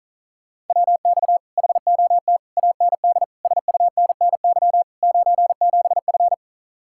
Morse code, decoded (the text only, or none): WX HOT AND SUNNY 97F